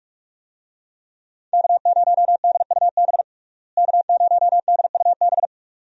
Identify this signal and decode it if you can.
Morse code, heard as a keyed tone, about 42 wpm, 700 Hz. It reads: K0DUB K0DUB
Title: USA Call Signs - 5 Characters - Repeated 2x - 45wpm